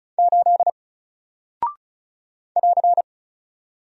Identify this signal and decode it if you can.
Morse code, heard as a keyed tone, about 36 wpm, 700 Hz. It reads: 8 <AR>
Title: Single Character (Letter:Number:Punctuation:Prosign) 35wpm